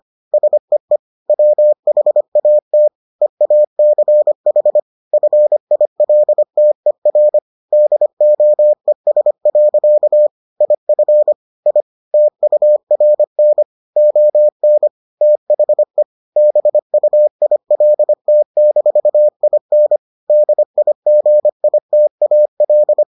Morse code, heard as a keyed tone, about 25 wpm, 600 Hz. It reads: SEE WHAT EACH FILTER DOES. IF I TURN ON THE BUILT-IN DIGITAL